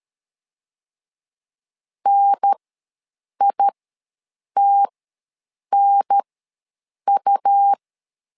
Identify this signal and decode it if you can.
Morse code, heard as a keyed tone, about 13 wpm, spaced as for 7 wpm, 800 Hz. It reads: NITNU
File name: LEZIONI / 2 UHDN / 2 UHDN ws 12wpm 1708